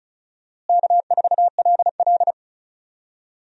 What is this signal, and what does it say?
Morse code, heard as a keyed tone, about 36 wpm, 700 Hz. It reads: K4LL
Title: USA Call Signs - 4 Characters - Repeated 3x - 35wpm